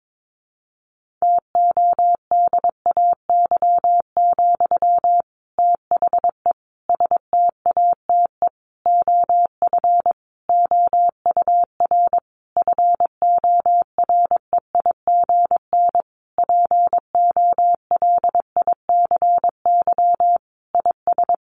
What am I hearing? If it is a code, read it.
Morse code, heard as a keyed tone, about 22 wpm, 700 Hz. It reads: TODAY, THE STATE OF OUR FOREIGN POLICY IS
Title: Sentences from Top 1400 Words 22wpm